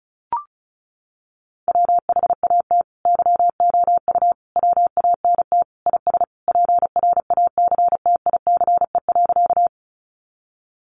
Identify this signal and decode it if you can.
Morse code, heard as a keyed tone, about 34 wpm, 700 Hz. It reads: WHAT YOU WANT IS PRACTICE.
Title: Sentences from Top 700 Words 35wpm